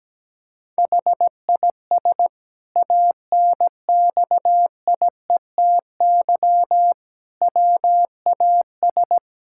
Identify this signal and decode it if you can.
Morse code, heard as a keyed tone, about 17 wpm, 700 Hz. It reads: HIS ANXIETY WAS
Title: Sentences from Top 1600 Words 17wpm